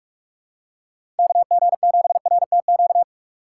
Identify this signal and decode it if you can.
Morse code, heard as a keyed tone, about 43 wpm, 700 Hz. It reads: KG7RTQ
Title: USA Call Signs - 6 Characters - 6x Character Spacing - 45wpm